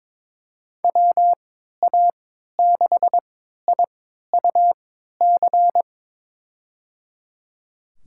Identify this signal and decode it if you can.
Morse code, heard as a keyed tone, about 22 wpm, spaced as for 13 wpm, 700 Hz. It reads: WA6IUC
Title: USA Call Signs - 6 Characters - 3x Character Spacing - 22wpm